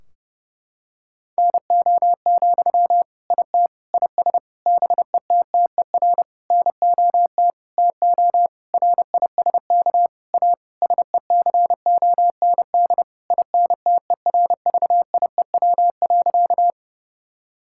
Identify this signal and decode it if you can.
Morse code, heard as a keyed tone, about 30 wpm, 700 Hz. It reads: NO, IT IS BETTER NOT TO RISK A SECOND INTERVIEW.
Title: Sentences from Top 1600 Words 30wpm